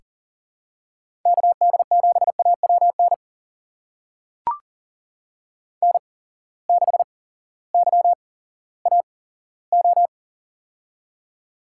Morse code, heard as a keyed tone, about 40 wpm, spaced as for 18 wpm, 700 Hz. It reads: KD7AWN N6YAO
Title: USA Call Signs - 8x Character Spacing - 40wpm